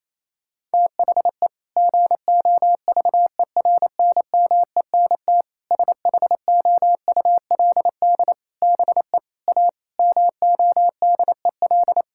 Morse code, heard as a keyed tone, about 27 wpm, 700 Hz. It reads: THE GOVERNMENT SHOULD BE A MODEL
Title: Sentences from Top 1600 Words 28wpm